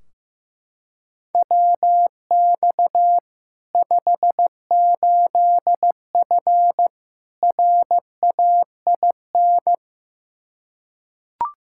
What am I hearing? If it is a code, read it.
Morse code, heard as a keyed tone, about 15 wpm, 700 Hz. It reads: WX 58F RAIN
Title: Practice QSO - Exchange 2 - Info - 15wpm